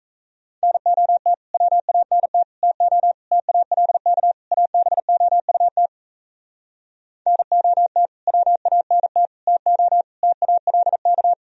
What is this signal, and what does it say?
Morse code, heard as a keyed tone, about 38 wpm, 700 Hz. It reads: NOT WANT TO TALK ABOUT NOT WANT TO TALK